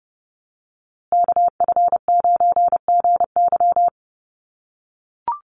Morse code, heard as a keyed tone, about 30 wpm, 700 Hz. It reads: KF9GY
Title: USA Call Signs Repeated 2x 30wpm